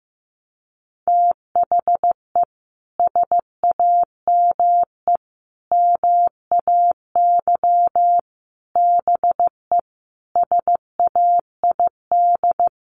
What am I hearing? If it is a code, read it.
Morse code, heard as a keyed tone, about 15 wpm, 700 Hz. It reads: THE SAME MAY BE SAID
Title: Sentences from Top 1100 Words 15wpm